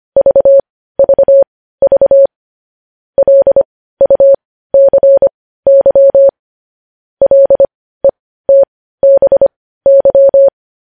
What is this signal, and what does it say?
Morse code, heard as a keyed tone, about 25 wpm, spaced as for 15 wpm, 550 Hz. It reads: VVV LUCY LETBY
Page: news headlines in morse code at 15 wpm